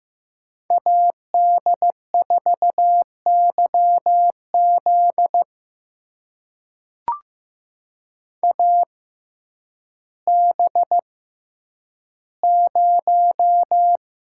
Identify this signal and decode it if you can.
Morse code, heard as a keyed tone, about 15 wpm, spaced as for 9 wpm, 700 Hz. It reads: AD4YZ AB0
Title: USA Call Signs - 5 Characters - 6x Character Spacing - 15wpm